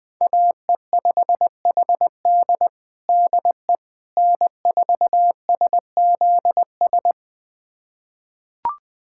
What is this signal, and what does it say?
Morse code, heard as a keyed tone, about 21 wpm, 700 Hz. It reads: AE5HD DE N4SZS
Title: USA Callsign DE Callsign - 5 Characters - 20wpm